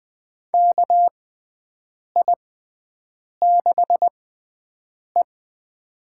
Morse code, heard as a keyed tone, about 20 wpm, spaced as for 7 wpm, 700 Hz. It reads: KI6E
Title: USA Call Signs - 4 Characters - 6x Character Spacing - 20wpm